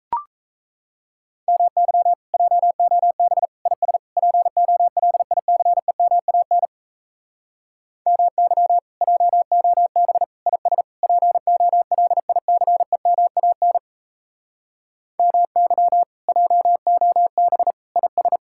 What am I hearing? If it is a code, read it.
Morse code, heard as a keyed tone, about 37 wpm, 700 Hz. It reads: MY JOB IS POLICEMAN MY JOB IS POLICEMAN MY JOB IS